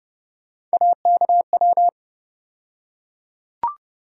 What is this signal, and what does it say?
Morse code, heard as a keyed tone, about 30 wpm, 700 Hz. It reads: AKW
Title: USA Call Sign Postfixes - 3 Letters 30wpm